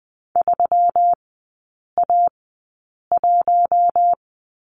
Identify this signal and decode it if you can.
Morse code, heard as a keyed tone, about 20 wpm, spaced as for 8 wpm, 690 Hz. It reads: 3A1
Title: 133 Lesson 22e - Question Mark - Three Characters 20wpm